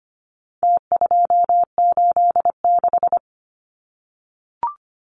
Morse code, heard as a keyed tone, about 25 wpm, 700 Hz. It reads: T286
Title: Numbers - Contest Style - 4-Digit 25wpm